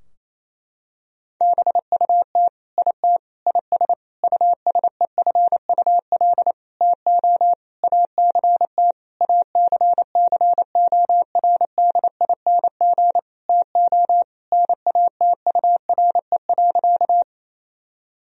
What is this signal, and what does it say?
Morse code, heard as a keyed tone, about 28 wpm, 700 Hz. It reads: BUT IT IS USEFUL TO ACT ACCORDING TO NATURE.
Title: Sentences from Top 1000 Words - Review Part 4 - 28wpm